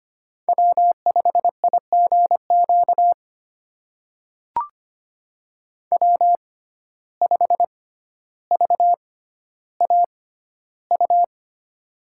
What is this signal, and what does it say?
Morse code, heard as a keyed tone, about 25 wpm, spaced as for 13 wpm, 700 Hz. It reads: W5IGQ W5VAU
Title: USA Call Signs - 5 Characters - 6x Character Spacing - 25wpm